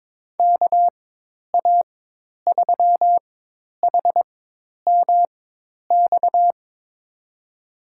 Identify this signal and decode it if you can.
Morse code, heard as a keyed tone, about 21 wpm, spaced as for 10 wpm, 700 Hz. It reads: KA3HMX
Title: USA Call Signs - 4x Character Spacing - 22wpm